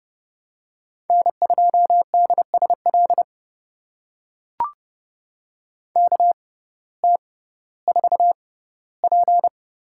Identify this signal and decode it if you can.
Morse code, heard as a keyed tone, about 29 wpm, spaced as for 16 wpm, 700 Hz. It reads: N2DSL KT4P
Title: USA Call Signs - 5 Characters - 6x Character Spacing - 30wpm